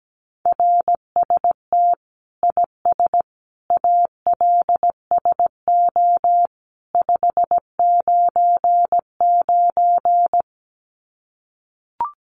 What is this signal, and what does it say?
Morse code, heard as a keyed tone, about 17 wpm, 700 Hz. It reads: RST IS ALSO 599
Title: Common QSO Phrases 17wpm